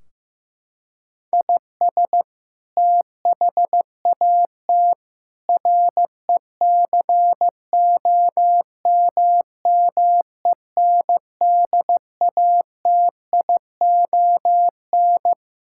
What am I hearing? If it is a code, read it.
Morse code, heard as a keyed tone, about 15 wpm, 700 Hz. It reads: IS THAT RECOMMENDATION
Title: Sentences from Top 1600 Words 15wpm